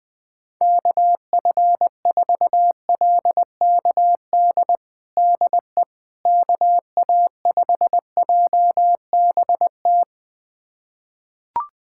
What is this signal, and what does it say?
Morse code, heard as a keyed tone, about 20 wpm, 700 Hz. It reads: KF4LKD DE KA5JBT
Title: USA Callsign DE Callsign - 6 Characters - 20wpm